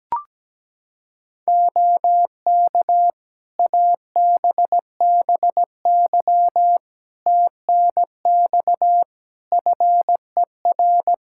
Morse code, heard as a keyed tone, about 17 wpm, 700 Hz. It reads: OK ABBY TNX FER
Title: Practice QSO - Exchange 4 - Ending - 17wpm